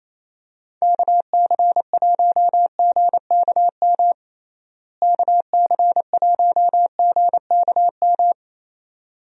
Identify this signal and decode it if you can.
Morse code, heard as a keyed tone, about 28 wpm, 700 Hz. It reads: KC1GKM KC1GKM
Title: USA Call Signs - 6 Characters - Repeated 2x - 28wpm